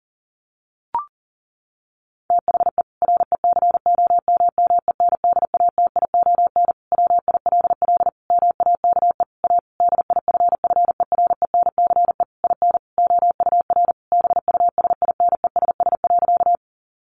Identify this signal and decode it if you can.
Morse code, heard as a keyed tone, about 40 wpm, 700 Hz. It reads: THE RECOMMENDATION WILL MAKE A DIFFERENCE IN OUR BUSINESS.
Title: Sentences from Top 1400 Words 40wpm